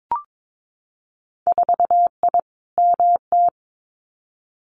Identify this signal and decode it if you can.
Morse code, heard as a keyed tone, about 22 wpm, 700 Hz. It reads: 4I MT
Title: Winter Field Day Exchanges Encore 22wpm